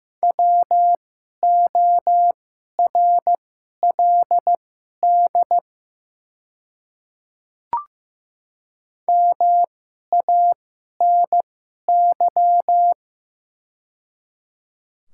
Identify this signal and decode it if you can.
Morse code, heard as a keyed tone, about 15 wpm, spaced as for 11 wpm, 700 Hz. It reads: WORLD MANY
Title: 2x Character Spacing - Top 200 Words - 15wpm